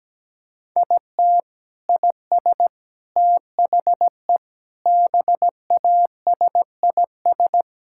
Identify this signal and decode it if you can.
Morse code, heard as a keyed tone, about 17 wpm, 700 Hz. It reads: IT IS THE BASIS
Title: Sentences from Top 1000 Words - Review Part 5 - 17wpm